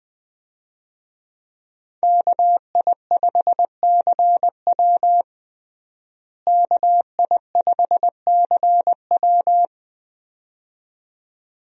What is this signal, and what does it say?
Morse code, heard as a keyed tone, about 20 wpm, 700 Hz. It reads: KI5CW KI5CW
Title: USA Call Signs - 5 Characters - Repeated 2x - 20wpm